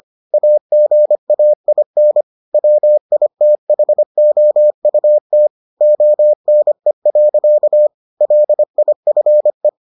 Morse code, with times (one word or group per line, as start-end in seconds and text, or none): 0.3-2.2 AGAIN
2.5-5.5 WITHOUT
5.8-7.9 ONE.
8.2-9.7 LIFE